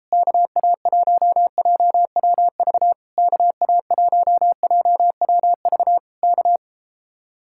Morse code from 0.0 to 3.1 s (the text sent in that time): KA1JWV